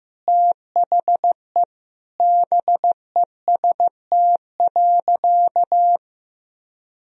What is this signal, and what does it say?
Morse code, heard as a keyed tone, about 15 wpm, 700 Hz. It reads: THE BEST.
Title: Sentences from Top 500 Words and Abbreviations 15wpm